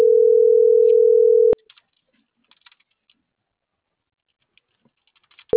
Ringback tone (North America), 440 + 480 Hz.